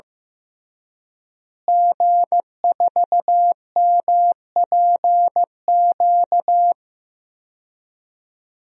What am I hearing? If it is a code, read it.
Morse code, heard as a keyed tone, about 16 wpm, 700 Hz. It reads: G4MPQ
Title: DXCC-100 Call Signs - 5 Characters - Repeated 3x - 15wpm